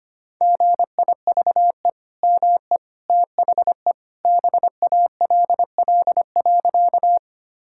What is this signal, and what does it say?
Morse code, heard as a keyed tone, about 25 wpm, 700 Hz. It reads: GIVE ME THE BALL.
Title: Sentences from Top 1000 Words - Review Part 6 - 25wpm